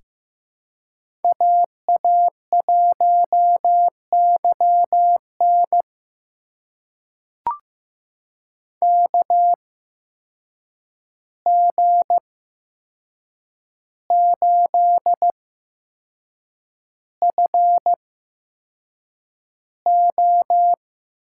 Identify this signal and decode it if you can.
Morse code, heard as a keyed tone, about 15 wpm, spaced as for 6 wpm, 700 Hz. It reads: AA1YN KG8FO
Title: USA Call Signs - 5 Characters - 8x Character Spacing - 15wpm